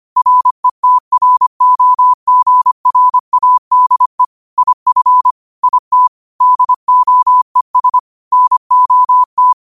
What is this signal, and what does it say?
Morse code, heard as a keyed tone, about 25 wpm, 1000 Hz. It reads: RETROGRADE IF IT DOES NOT